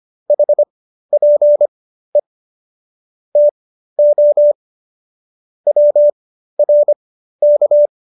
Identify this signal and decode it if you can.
Morse code, heard as a keyed tone, about 25 wpm, spaced as for 13 wpm, 590 Hz. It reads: HPE TO WRK